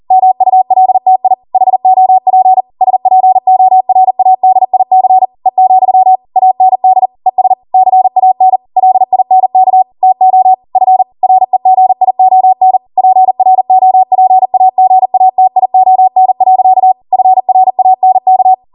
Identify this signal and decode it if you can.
Morse code, heard as a keyed tone, about 40 wpm, 750 Hz. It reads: MULTI HOP SPORADIC E, AND ES CAN LINK TO F REGION PROPAGATION. FRANK